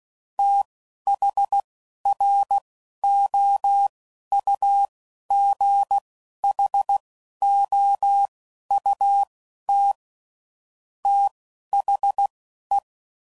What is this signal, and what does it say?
Morse code, heard as a keyed tone, about 16 wpm, spaced as for 12 wpm, 800 Hz. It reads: THROUGHOUT THE
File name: CW Code Groups @10WPM / Weather Sets / Weather 5 12